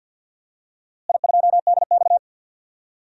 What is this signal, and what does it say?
Morse code, heard as a keyed tone, about 52 wpm, 700 Hz. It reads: I2DX
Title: DXCC-100 Call Signs - 4 Characters - Repeated 2x - 50wpm